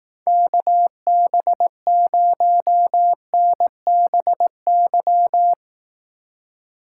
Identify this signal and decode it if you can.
Morse code, heard as a keyed tone, about 17 wpm, 700 Hz. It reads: KB0NBY
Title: Speed Racing - USA 2x3 Call Signs 15wpm